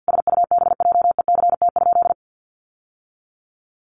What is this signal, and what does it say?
Morse code, heard as a keyed tone, about 50 wpm, 700 Hz. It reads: SUBJECT?
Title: Sentences from Top 800 Words 50wpm